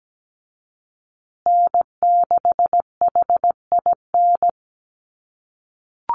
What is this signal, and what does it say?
Morse code, heard as a keyed tone, about 17 wpm, 700 Hz. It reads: N6HIN
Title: USA Call Signs Repeated 2x 17wpm